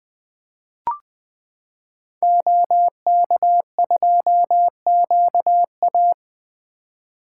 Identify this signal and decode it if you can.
Morse code, heard as a keyed tone, about 20 wpm, 700 Hz. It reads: OK2QA
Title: 2023 SOTA Activator Call Signs - 20wpm